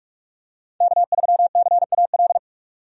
Morse code, heard as a keyed tone, about 44 wpm, 700 Hz. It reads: K3CAL
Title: USA Call Signs - 5 Characters - 45wpm